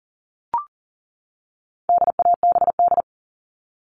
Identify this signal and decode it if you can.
Morse code, heard as a keyed tone, about 40 wpm, 700 Hz. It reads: DABD
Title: All Cut Numbers - 4-Digit 40wpm